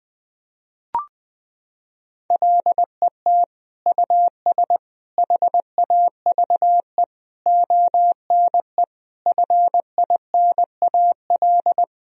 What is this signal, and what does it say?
Morse code, heard as a keyed tone, about 21 wpm, 700 Hz. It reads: LET US HAVE ONE FINAL
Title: Sentences from Top 1000 Words - Review Part 2 - 20wpm